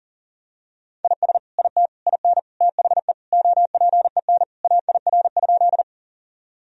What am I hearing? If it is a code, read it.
Morse code, heard as a keyed tone, about 40 wpm, 700 Hz. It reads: IS IT IN THE OPEN AIR?
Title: Sentences from Top 1000 Words - Review Part 1 - 40wpm